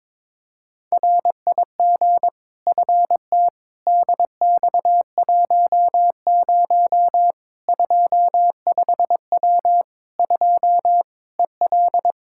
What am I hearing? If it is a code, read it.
Morse code, heard as a keyed tone, about 22 wpm, 700 Hz. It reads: RIG FT DX10 25W 2 EL
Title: Practice QSO - Exchange 2 - Info - 22wpm